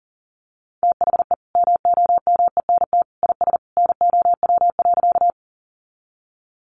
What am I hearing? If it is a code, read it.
Morse code, heard as a keyed tone, about 37 wpm, 700 Hz. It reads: THE MOMENT IS NOW.
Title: Sentences from Top 1000 Words 40wpm